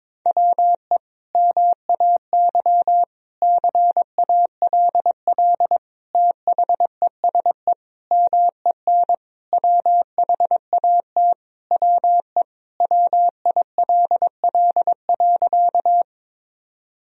Morse code, heard as a keyed tone, about 21 wpm, 700 Hz. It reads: WE MAY CALL THESE MEN WHAT WE WILL.
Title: Sentences from Top 1000 Words - Review Part 1 - 22wpm